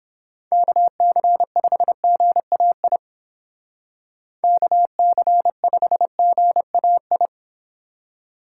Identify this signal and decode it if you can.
Morse code, heard as a keyed tone, about 31 wpm, 700 Hz. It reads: KC5GAI KC5GAI